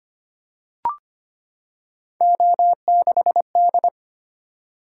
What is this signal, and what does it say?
Morse code, heard as a keyed tone, about 25 wpm, 700 Hz. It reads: O6D
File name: Three Letter-Number Combinations - 25wpm